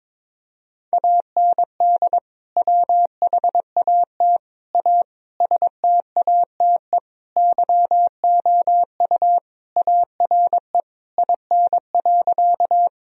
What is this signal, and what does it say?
Morse code, heard as a keyed tone, about 22 wpm, 700 Hz. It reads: AND WHAT A STATE YOU ARE IN.
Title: Sentences from Top 1000 Words - Review Part 2 - 22wpm